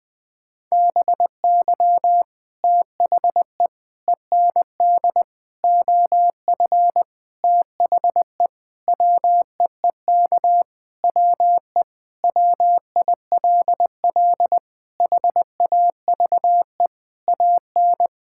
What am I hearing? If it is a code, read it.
Morse code, heard as a keyed tone, about 20 wpm, 700 Hz. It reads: BY THE END OF THE WEEK WE WILL HAVE AN
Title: Sentences with 5-Word Phrases - Spoken Beginning - 20wpm